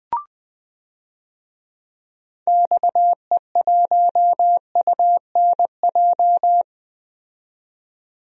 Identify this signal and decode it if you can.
Morse code, heard as a keyed tone, about 20 wpm, 700 Hz. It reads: XE1UNJ
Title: DXCC-100 Call Signs - 6 Characters - 20wpm